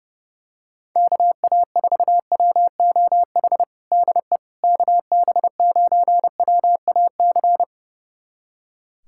Morse code, heard as a keyed tone, about 30 wpm, 700 Hz. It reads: KA4WOH DE KB9WAC
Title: USA Callsign DE Callsign 30wpm